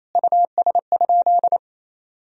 Morse code, heard as a keyed tone, about 28 wpm, 700 Hz. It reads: US?
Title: Sentences from Top 1000 Words - Review Part 3 - 28wpm